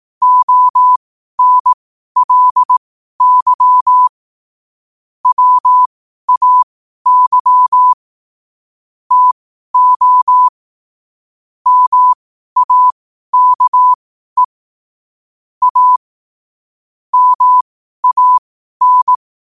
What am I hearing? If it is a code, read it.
Morse code, heard as a keyed tone, about 18 wpm, spaced as for 12 wpm, 1000 Hz. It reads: ONLY WAY TO MAKE A MAN
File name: qotdfiles / 12wpm